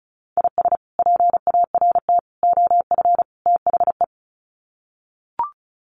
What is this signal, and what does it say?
Morse code, heard as a keyed tone, about 34 wpm, 700 Hz. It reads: IS PART OF THE
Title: Sets of 4 Words 35wpm